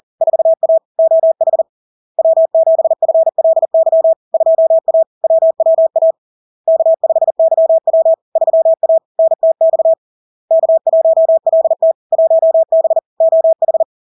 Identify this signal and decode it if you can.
Morse code, heard as a keyed tone, about 40 wpm, 650 Hz. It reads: 4A OH W7FLY 2A WWA K5YW 3A NTX K1LT 1B OH
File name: LARK-FD-2022-Log-in-CW-40wpm-650-Hz